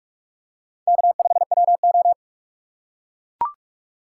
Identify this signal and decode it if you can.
Morse code, heard as a keyed tone, about 44 wpm, 700 Hz. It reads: K5WO
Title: USA Call Signs - 4 Characters - 6x Character Spacing - 45wpm